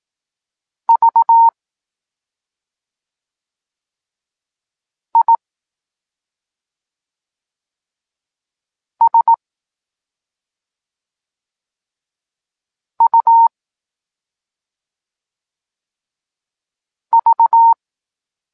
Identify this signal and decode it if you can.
Morse code, heard as a keyed tone, about 18 wpm, spaced as for 2 wpm, 920 Hz. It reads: VISUV
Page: Tekst 8 – doskonalenie odbioru poznanych liter